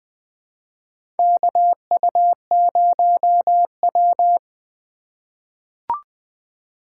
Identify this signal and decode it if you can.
Morse code, heard as a keyed tone, about 20 wpm, 700 Hz. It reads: KU0W
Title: USA Call Signs - 4 Characters - 2x Character Spacing - 20wpm